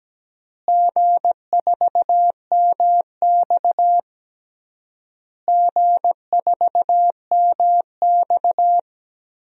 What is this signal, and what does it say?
Morse code, heard as a keyed tone, about 17 wpm, 700 Hz. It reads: G4MX G4MX
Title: DXCC-100 Call Signs - 4 Characters - Repeated 3x - 17wpm